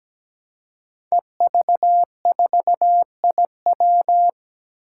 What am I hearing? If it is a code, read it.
Morse code, heard as a keyed tone, about 17 wpm, 700 Hz. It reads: EV4IW
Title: Five Challenging Letter-Number Combinations 17wpm